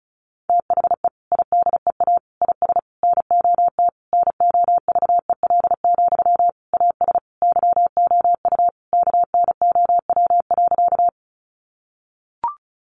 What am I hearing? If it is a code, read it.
Morse code, heard as a keyed tone, about 34 wpm, 700 Hz. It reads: THE IDEA IS NOT NOVEL, AS YOU KNOW.
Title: Sentences from Top 1300 Words 35wpm